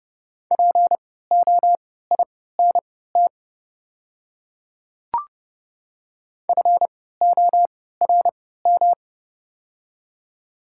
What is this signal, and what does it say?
Morse code, heard as a keyed tone, about 29 wpm, spaced as for 17 wpm, 690 Hz. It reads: POINT FORM
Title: 3x Character Spacing - Top 200 Words - 30wpm